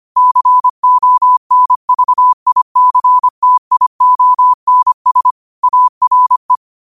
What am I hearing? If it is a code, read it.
Morse code, heard as a keyed tone, about 25 wpm, 1000 Hz. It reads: CONVICTIONS ARE